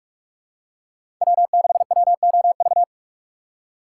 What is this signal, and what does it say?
Morse code, heard as a keyed tone, about 46 wpm, 690 Hz. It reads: W6WOV
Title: USA Call Signs - 5 Characters - 6x Character Spacing - 45wpm